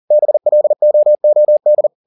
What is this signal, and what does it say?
Morse code, heard as a keyed tone, about 40 wpm, 600 Hz. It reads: BLOOD
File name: the old man and the sea 2 40wpm